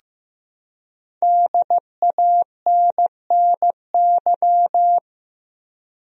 Morse code, heard as a keyed tone, about 15 wpm, 700 Hz. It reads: DANNY
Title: USA Common Names - 15wpm